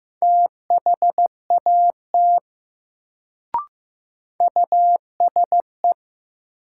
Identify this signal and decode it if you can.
Morse code, heard as a keyed tone, about 15 wpm, 700 Hz. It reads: THAT USE
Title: Sentences from Top 100 Words - New Word Drill - 15wpm